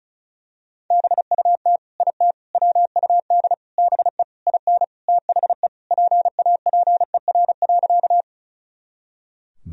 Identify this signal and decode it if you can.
Morse code, heard as a keyed tone, about 36 wpm, 700 Hz. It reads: BUT IT WUD BE IN THE PAPER.
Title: Sentences from Top 300 Words and Abbreviations 35wpm